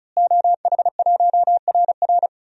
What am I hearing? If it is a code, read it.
Morse code, heard as a keyed tone, about 36 wpm, 700 Hz. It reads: OH1RR
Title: DXCC-100 Call Signs - 5 Characters - Repeated 3x - 35wpm